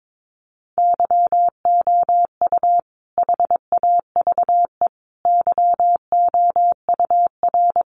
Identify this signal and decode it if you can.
Morse code, heard as a keyed tone, about 22 wpm, 700 Hz. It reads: YOU HAVE YOUR
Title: Sentences from Top 400 Words 22wpm